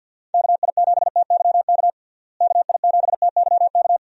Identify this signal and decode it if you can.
Morse code, heard as a keyed tone, about 51 wpm, 700 Hz. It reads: KI7TYK KI7TYK
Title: USA Call Signs - 6 Characters - Repeated 2x - 50wpm